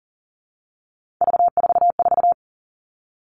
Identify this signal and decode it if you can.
Morse code, heard as a keyed tone, about 40 wpm, 700 Hz. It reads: V44
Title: All Cut Numbers - 3-Digit 40wpm